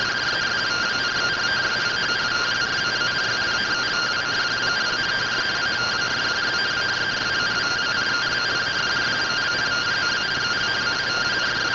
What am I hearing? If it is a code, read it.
Two-tone teleprinter signal, about 170 Hz shift.
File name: findings / SAGITAL fr15 nb203 3mm.gen-39 rtty rx